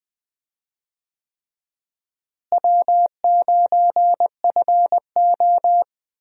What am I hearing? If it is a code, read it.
Morse code, heard as a keyed tone, about 21 wpm, 700 Hz. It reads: W9FO
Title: USA Call Signs - 4 Characters - Repeated 3x - 20wpm